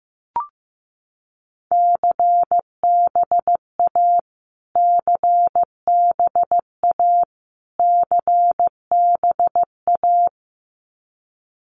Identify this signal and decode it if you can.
Morse code, heard as a keyed tone, about 15 wpm, 700 Hz. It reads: CBA CBA CBA
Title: Abbreviations Repeated 3x - 15wpm